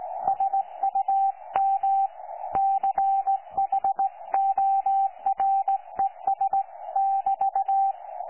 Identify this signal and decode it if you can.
Morse code, heard as a keyed tone, about 16 wpm, 780 Hz. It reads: SUM CHORES<BT>
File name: 12 39Z 18088.2kHz utah aa7fv k9tjl